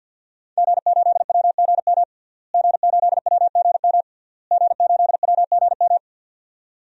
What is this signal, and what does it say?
Morse code, heard as a keyed tone, about 52 wpm, 700 Hz. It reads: G8WGM G8WGM G8WGM
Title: DXCC-100 Call Signs - 5 Characters - Repeated 3x - 50wpm